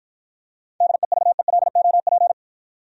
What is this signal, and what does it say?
Morse code, heard as a keyed tone, about 53 wpm, 700 Hz. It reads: DEVELOP